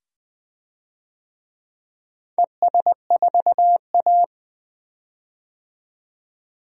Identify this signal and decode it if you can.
Morse code, heard as a keyed tone, about 20 wpm, 700 Hz. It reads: ES4A
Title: DXCC-100 Call Signs - 4 Characters - 20wpm